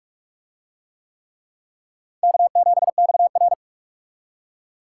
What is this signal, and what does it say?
Morse code, heard as a keyed tone, about 42 wpm, 700 Hz. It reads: K7XR